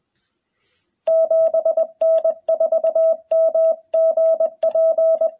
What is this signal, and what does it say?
Morse code, heard as a keyed tone, about 20 wpm, 650 Hz, with radio noise. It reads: 7N4MGP
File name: jh8cbh / 2021cw